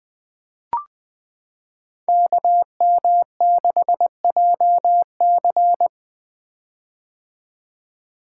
Morse code, heard as a keyed tone, about 20 wpm, 700 Hz. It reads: KM6JC
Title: Code Groups - 5 Letter:Numbers - 20wpm